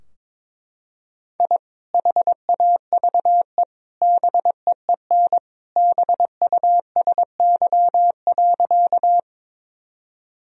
Morse code, heard as a keyed tone, about 23 wpm, 700 Hz. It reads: I HAVE BEEN BUSY.
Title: Sentences from Top 1000 Words - Review Part 3 - 22wpm